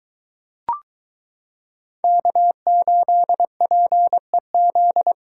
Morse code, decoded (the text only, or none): K8PEZ